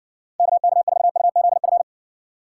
Morse code, heard as a keyed tone, about 67 wpm, 700 Hz. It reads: KK4UZF